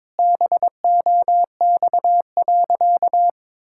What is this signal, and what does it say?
Morse code, heard as a keyed tone, about 22 wpm, 700 Hz. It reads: BOX.
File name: practice / 22wpm / Sentences with 3-Word Phrases - 22wpm